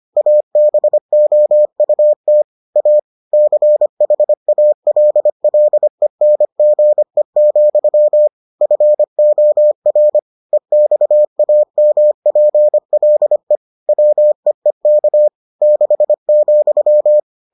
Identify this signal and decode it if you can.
Morse code, heard as a keyed tone, about 25 wpm, 600 Hz. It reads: ABOUT A CHALLENGE, FOR EXAMPLE WEEK 6,